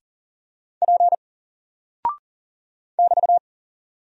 Morse code, heard as a keyed tone, about 40 wpm, 700 Hz. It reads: P <BT>
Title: Single Character (Letter:Number:Punctuation:Prosign) Alt 40wpm